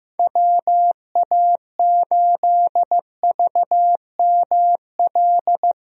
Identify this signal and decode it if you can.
Morse code, heard as a keyed tone, about 15 wpm, 700 Hz. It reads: WA8VML
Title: USA Call Signs - 6 Characters - Repeated 3x - 15wpm